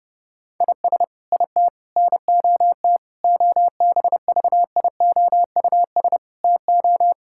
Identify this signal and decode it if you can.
Morse code, heard as a keyed tone, about 30 wpm, 700 Hz. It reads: IS IT NOT OBVIOUS TO
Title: Sentences from Top 1600 Words 30wpm